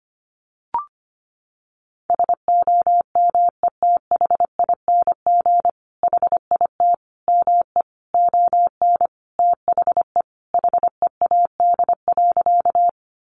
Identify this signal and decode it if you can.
Morse code, heard as a keyed tone, about 25 wpm, 700 Hz. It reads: SOMETHING HIT ME ON THE HEAD.
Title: Sentences from Top 1200 Words 25wpm